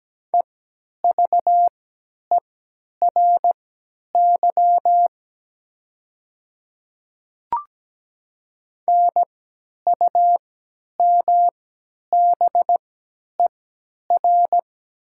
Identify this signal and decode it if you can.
Morse code, heard as a keyed tone, about 17 wpm, spaced as for 10 wpm, 700 Hz. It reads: EVERY NUMBER
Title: 3x Character Spacing - Top 300 Words - 17wpm